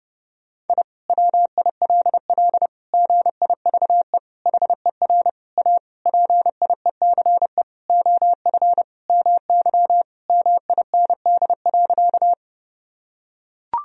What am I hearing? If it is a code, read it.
Morse code, heard as a keyed tone, about 30 wpm, 700 Hz. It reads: I WILL GIVE HER A PIECE OF MY MIND.
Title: Sentences from Top 1000 Words - Review Part 2 - 30wpm